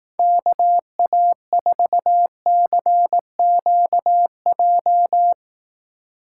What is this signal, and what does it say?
Morse code, heard as a keyed tone, about 18 wpm, 700 Hz. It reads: KA4CQJ
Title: Speed Racing - USA 2x3 Call Signs 15wpm